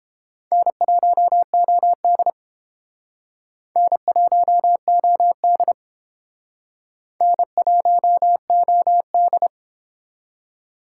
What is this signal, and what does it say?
Morse code, heard as a keyed tone, about 29 wpm, 700 Hz. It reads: N1OD N1OD N1OD